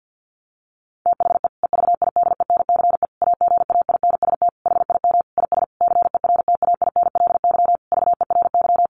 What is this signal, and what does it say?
Morse code, heard as a keyed tone, about 50 wpm, 700 Hz. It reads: THE EVIDENCE AGAINST HIM IS CERTAINLY VERY
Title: Sentences from Top 1300 Words 50wpm